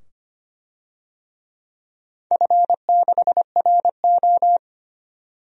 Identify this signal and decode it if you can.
Morse code, heard as a keyed tone, about 25 wpm, 700 Hz. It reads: F6RO
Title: DXCC-100 Call Signs - 4 Characters - 25wpm